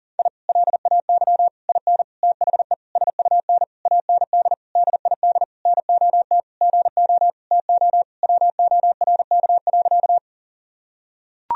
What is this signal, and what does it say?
Morse code, heard as a keyed tone, about 37 wpm, 700 Hz. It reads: I LAY IN THE SUN AND DID NOT GO TO WORK.
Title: Sentences from Top 1000 Words - Review Part 2 - 40wpm